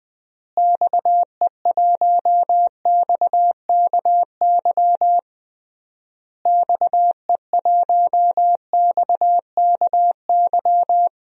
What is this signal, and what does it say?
Morse code, heard as a keyed tone, about 20 wpm, 700 Hz. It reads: XE1XKY XE1XKY
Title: DXCC-100 Call Signs - 6 Characters - Repeated 3x - 20wpm